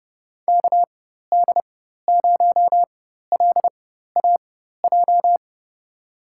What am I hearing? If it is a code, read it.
Morse code, heard as a keyed tone, about 30 wpm, spaced as for 14 wpm, 700 Hz. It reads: KD0LAJ
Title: USA Call Signs - 4x Character Spacing - 30wpm